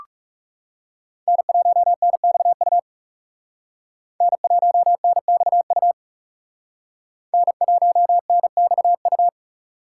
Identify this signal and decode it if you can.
Morse code, heard as a keyed tone, about 37 wpm, 700 Hz. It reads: N1NXU N1NXU N1NXU
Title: Speed Racing - USA 1x3 Call Signs 30wpm